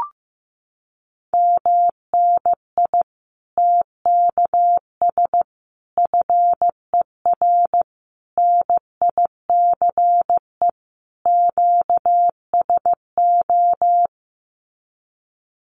Morse code, heard as a keyed tone, about 15 wpm, 700 Hz. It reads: MNI TKS FER NICE QSO
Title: Common QSO Phrases 15wpm